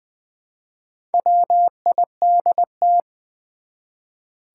Morse code, heard as a keyed tone, about 20 wpm, 700 Hz. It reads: WIDT
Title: Build A Word - 1 Letter At A Time - 5-Letter Words - 20wpm